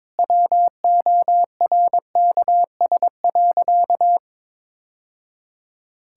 Morse code, heard as a keyed tone, about 22 wpm, 700 Hz. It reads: WORKS.
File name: practice / 22wpm / Sentences with 2-Word Phrases - 22wpm